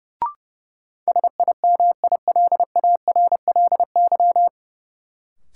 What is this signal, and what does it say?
Morse code, heard as a keyed tone, about 32 wpm, 700 Hz. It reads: SIMILARLY
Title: Sentences from Top 1400 Words - New Word Drill Alt - 30wpm